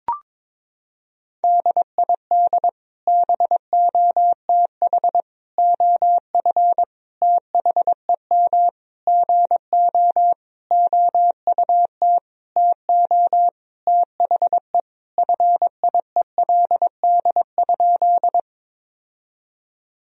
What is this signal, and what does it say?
Morse code, heard as a keyed tone, about 22 wpm, 700 Hz. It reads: DID BOTH OF THEM GO OUT TO THE FIELD?
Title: Sentences from Top 1000 Words - Review Part 2 - 22wpm